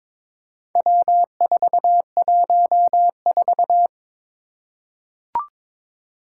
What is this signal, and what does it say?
Morse code, heard as a keyed tone, about 22 wpm, 700 Hz. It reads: W414
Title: Four Challenging Letter-Number Combinations 22wpm